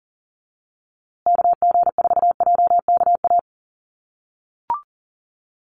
Morse code, heard as a keyed tone, about 40 wpm, 700 Hz. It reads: KG4JKA
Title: USA Call Signs Repeated 2x 40wpm